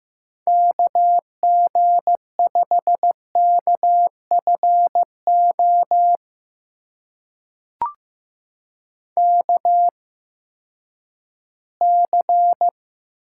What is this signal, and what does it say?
Morse code, heard as a keyed tone, about 15 wpm, spaced as for 10 wpm, 700 Hz. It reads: KG5KFO KC
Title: USA Call Signs - 6 Characters - 8x Character Spacing - 15wpm